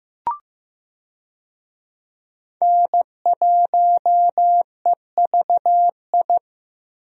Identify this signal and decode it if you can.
Morse code, heard as a keyed tone, about 15 wpm, 700 Hz. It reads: N1EVI
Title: USA Call Signs - 5 Characters - 15wpm